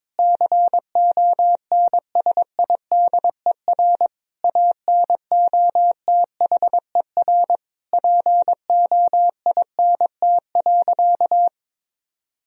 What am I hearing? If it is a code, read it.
Morse code, heard as a keyed tone, about 22 wpm, 700 Hz. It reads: CONSIDER ANOTHER POINT.
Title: Sentences from Top 1000 Words - Review Part 4 - 22wpm